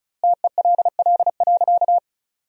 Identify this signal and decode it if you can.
Morse code, heard as a keyed tone, about 34 wpm, 700 Hz. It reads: TELL.
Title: 4x Word-Spacing - Sentences from Top 300 Words - 35wpm